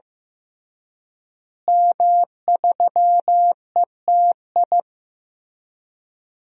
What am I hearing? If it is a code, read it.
Morse code, heard as a keyed tone, about 15 wpm, 700 Hz. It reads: M3ETI
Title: DXCC-100 Call Signs - 5 Characters - Repeated 3x - 15wpm